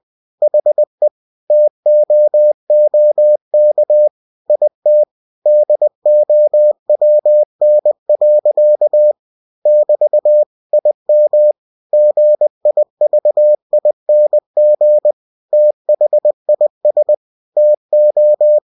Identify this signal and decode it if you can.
Morse code, heard as a keyed tone, about 20 wpm, 600 Hz. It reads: HE TOOK IT DOWN. <BT> IM GIVING THIS TO
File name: cw-books / winnie-the-pooh / winnie-20wpm-page-0049